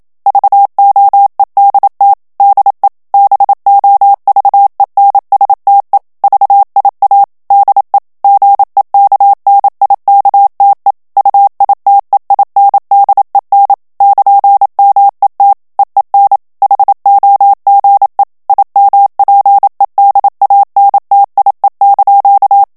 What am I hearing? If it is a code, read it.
Morse code, heard as a keyed tone, about 27 wpm, 800 Hz. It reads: VOEDT DE BOVENSTE VIA DE GEKNIKTE UITEINDEN <KN>MET EEN HOGE IMPEDANTIE)
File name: HOLLANDAIS texte clair 1 vitesse 26 mots minute